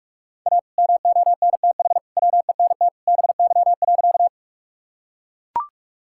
Morse code, heard as a keyed tone, about 43 wpm, 700 Hz. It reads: A MONTH WENT BY.
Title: Sentences from Top 1000 Words - Review Part 5 - 45wpm